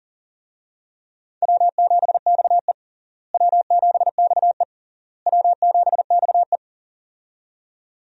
Morse code, heard as a keyed tone, about 40 wpm, 700 Hz. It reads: W7XE W7XE W7XE
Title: USA Call Signs - 4 Characters - Repeated 3x - 40wpm